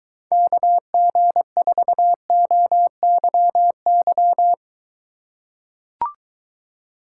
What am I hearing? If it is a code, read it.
Morse code, heard as a keyed tone, about 22 wpm, 700 Hz. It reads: KG4OYY